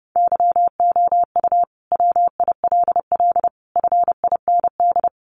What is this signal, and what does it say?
Morse code, heard as a keyed tone, about 28 wpm, 700 Hz. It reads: YOU WILL FIND
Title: Sentences from Top 1100 Words 30wpm